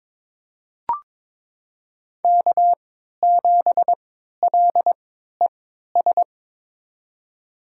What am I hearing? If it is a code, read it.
Morse code, heard as a keyed tone, about 21 wpm, spaced as for 12 wpm, 700 Hz. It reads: K7LES
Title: USA Call Signs - 3x Character Spacing - 22wpm